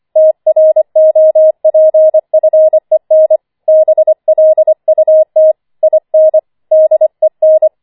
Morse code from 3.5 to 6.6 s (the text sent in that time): BLUT IN